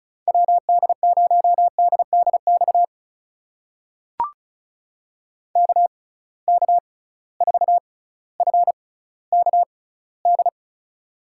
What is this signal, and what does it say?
Morse code, heard as a keyed tone, about 36 wpm, spaced as for 18 wpm, 700 Hz. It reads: WD0DDX KK4FKD
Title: USA Call Signs - 6 Characters - 6x Character Spacing - 35wpm